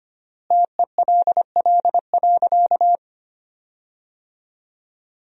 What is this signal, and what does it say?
Morse code, heard as a keyed tone, about 25 wpm, 700 Hz. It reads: TELL.
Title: 4x Word-Spacing - Sentences from Top 300 Words - 25wpm